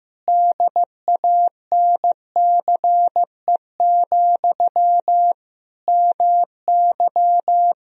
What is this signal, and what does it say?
Morse code, heard as a keyed tone, about 15 wpm, 700 Hz. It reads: DANCE, MY
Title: Sentences from Top 1000 Words - Review Part 4 - 15wpm